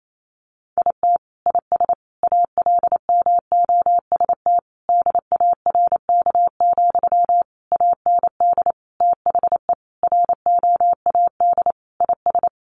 Morse code, heard as a keyed tone, about 28 wpm, 700 Hz. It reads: IT IS ALMOST DARK, AND THE ROAD IS
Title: Sentences from Top 1200 Words 28wpm